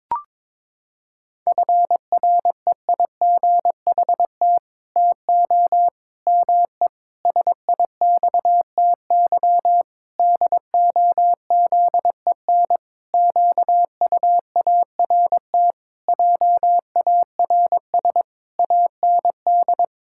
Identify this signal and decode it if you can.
Morse code, heard as a keyed tone, about 22 wpm, 700 Hz. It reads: FREIGHT TO ME SIXTY DOZEN QUART JARS AND
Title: Pangram Challenge v2 - 22wpm